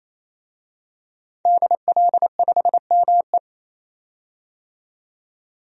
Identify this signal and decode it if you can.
Morse code, heard as a keyed tone, about 28 wpm, 700 Hz. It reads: DL5ME
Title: DXCC-100 Call Signs - 5 Characters - 28wpm